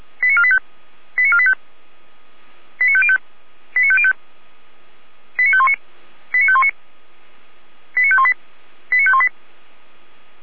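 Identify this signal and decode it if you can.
Five-tone selective call (ZVEI), heard as quick runs of five tones, tones of about 70 ms. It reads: 89485 89485 89505 89505 89519 89519 89518 89518
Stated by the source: Five-tone ZVEI selective call 89380, received by SDR.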